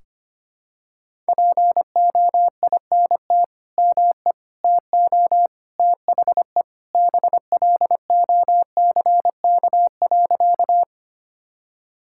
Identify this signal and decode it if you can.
Morse code, heard as a keyed tone, about 25 wpm, 700 Hz. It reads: POINT ME TO THE BLOCK.
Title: Sentences from Top 1600 Words 25wpm